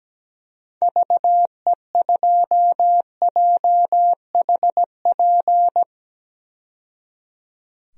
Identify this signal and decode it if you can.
Morse code, heard as a keyed tone, about 17 wpm, 700 Hz. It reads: VE2JHP
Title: DXCC-100 Call Signs - 6 Characters - 17wpm